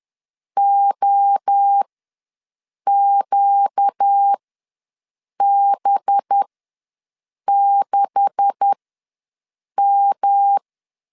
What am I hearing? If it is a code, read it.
Morse code, heard as a keyed tone, about 10 wpm, spaced as for 6 wpm, 800 Hz. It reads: OQB6M
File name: LEZIONI / 10 SIGNS / 10B INT ws 10wpm 4134